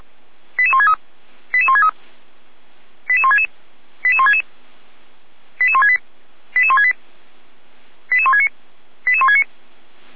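Five-tone selective call (ZVEI), heard as quick runs of five tones, tones of about 70 ms. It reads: 88163 88163 88166 88166 88168 88168 88169 88169
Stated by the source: ZVEI five-tone call 88305 received on an SDR.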